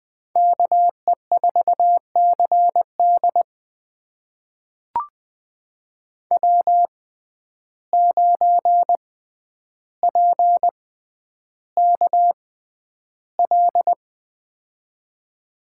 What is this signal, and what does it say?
Morse code, heard as a keyed tone, about 20 wpm, spaced as for 10 wpm, 700 Hz. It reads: KE4CD W9PKL
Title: USA Call Signs - 5 Characters - 6x Character Spacing - 20wpm